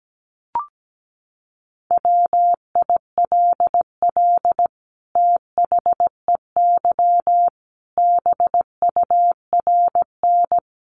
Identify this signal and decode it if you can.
Morse code, heard as a keyed tone, about 17 wpm, 700 Hz. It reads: WILL THEY BURN
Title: Sentences from Top 1300 Words 17wpm